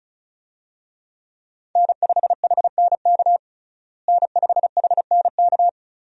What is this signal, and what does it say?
Morse code, heard as a keyed tone, about 34 wpm, 700 Hz. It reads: N5HNK N5HNK
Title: USA Call Signs - 5 Characters - Repeated 2x - 35wpm